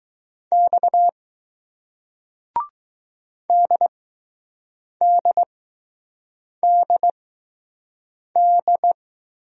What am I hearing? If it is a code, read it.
Morse code, heard as a keyed tone, about 20 wpm, spaced as for 6 wpm, 700 Hz. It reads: X DDDD